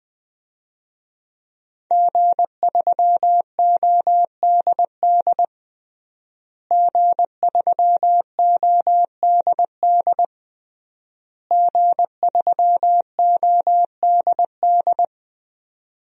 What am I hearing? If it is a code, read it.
Morse code, heard as a keyed tone, about 20 wpm, 700 Hz. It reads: G3ODD G3ODD G3ODD
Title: DXCC-100 Call Signs - 5 Characters - Repeated 3x - 20wpm